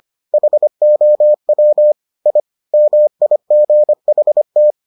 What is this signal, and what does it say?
Morse code, heard as a keyed tone, about 25 wpm, 600 Hz. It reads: HOW I MIGHT